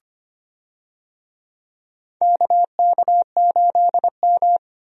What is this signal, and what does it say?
Morse code, heard as a keyed tone, about 25 wpm, 700 Hz. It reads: KK8M
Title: USA Call Signs - 4 Characters - Repeated 3x - 25wpm